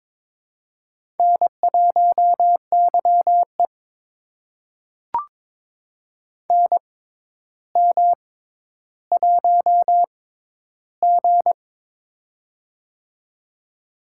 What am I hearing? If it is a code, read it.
Morse code, heard as a keyed tone, about 22 wpm, spaced as for 11 wpm, 700 Hz. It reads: N1YE NM1G
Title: USA Call Signs - 4 Characters - 6x Character Spacing - 22wpm